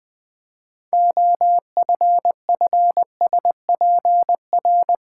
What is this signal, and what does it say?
Morse code, heard as a keyed tone, about 20 wpm, 700 Hz. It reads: OFFSPR
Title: Build A Word - 3 Letters At A Time - 9-Letter Words - 20wpm